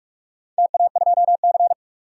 Morse code, heard as a keyed tone, about 46 wpm, 700 Hz. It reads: TA2C
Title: DXCC-100 Call Signs - 4 Characters - 45wpm